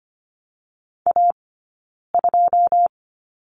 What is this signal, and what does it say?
Morse code, heard as a keyed tone, about 25 wpm, 700 Hz. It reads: A 2
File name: Lesson 15d - Number 5 - Two Characters 25wpm